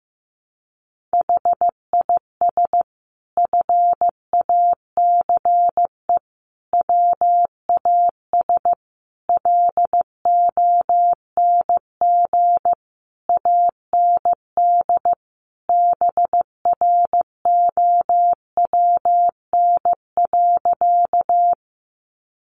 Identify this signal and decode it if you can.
Morse code, heard as a keyed tone, about 15 wpm, 700 Hz. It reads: HIS FACE WAS LONG AND BROWN.
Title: Sentences from Top 1200 Words 15wpm